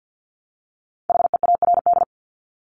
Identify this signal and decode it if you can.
Morse code, heard as a keyed tone, about 50 wpm, 700 Hz. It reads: HEARD
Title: Top 500 Words 50wpm